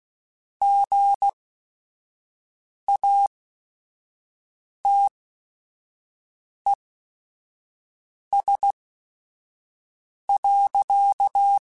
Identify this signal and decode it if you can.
Morse code, heard as a keyed tone, about 16 wpm, spaced as for 5 wpm, 800 Hz. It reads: GATES.
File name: Technical Subjects